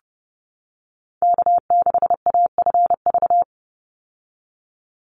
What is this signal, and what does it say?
Morse code, heard as a keyed tone, about 30 wpm, 700 Hz. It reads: K6AFV
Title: USA Call Signs 30wpm